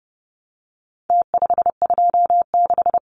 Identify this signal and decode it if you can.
Morse code, heard as a keyed tone, about 30 wpm, 700 Hz. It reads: T526
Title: Numbers - Contest Style - 4-Digit 30wpm